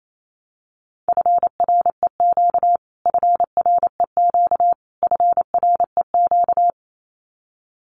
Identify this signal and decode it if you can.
Morse code, heard as a keyed tone, about 28 wpm, 700 Hz. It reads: FREQ FREQ FREQ
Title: Abbreviations Repeated 3x - 28wpm